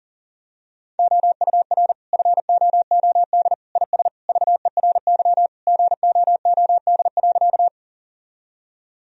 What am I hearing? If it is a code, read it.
Morse code, heard as a keyed tone, about 37 wpm, 700 Hz. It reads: OUR FOOD IS VERY GOOD.
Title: Sentences from Top 1000 Words - Review Part 1 - 40wpm